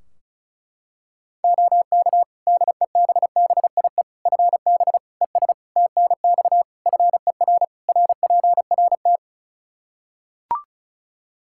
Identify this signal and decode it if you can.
Morse code, heard as a keyed tone, about 36 wpm, 700 Hz. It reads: OK DEBBIE FB ES TNX FER RPRT
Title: Practice QSO - Exchange 2 - Info - 35wpm